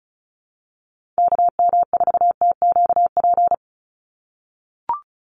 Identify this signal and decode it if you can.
Morse code, heard as a keyed tone, about 34 wpm, 700 Hz. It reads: KM4TQP
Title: USA Call Signs 35wpm